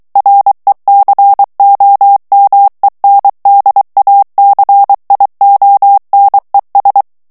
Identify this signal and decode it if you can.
Morse code, heard as a keyed tone, about 23 wpm, 800 Hz. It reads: RECOMENDACIONES
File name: morse mp3 / ESPAGNOL TEXTE EN CLAIR 2 VITESSE 22 MOTS MINUTES